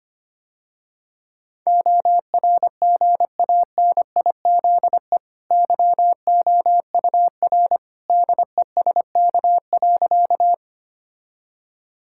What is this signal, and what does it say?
Morse code, heard as a keyed tone, about 25 wpm, 700 Hz. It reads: ORGANIZE YOUR DESK.
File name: Easy Sentences with Hints - 25wpm